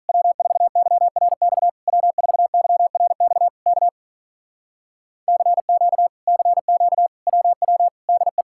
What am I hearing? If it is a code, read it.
Morse code, heard as a keyed tone, about 47 wpm, 700 Hz. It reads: W4YRX W4YRX K CQ CQ WW DE